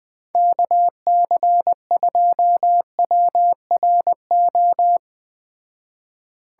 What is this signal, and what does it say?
Morse code, heard as a keyed tone, about 20 wpm, 700 Hz. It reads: KC2WRO
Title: Speed Racing - USA 2x3 Call Signs 17wpm